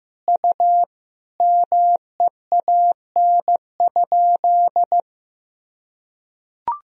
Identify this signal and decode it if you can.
Morse code, heard as a keyed tone, about 16 wpm, 700 Hz. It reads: U MEAN?
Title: Sentences from Top 200 Words and Abbreviations 15wpm